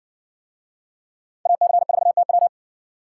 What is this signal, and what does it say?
Morse code, heard as a keyed tone, about 60 wpm, 700 Hz. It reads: AC4TW